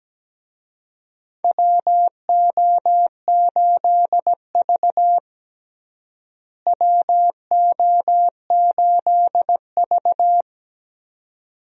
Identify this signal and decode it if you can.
Morse code, heard as a keyed tone, about 17 wpm, 700 Hz. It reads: WO8V WO8V
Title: USA Call Signs - 4 Characters - Repeated 2x - 17wpm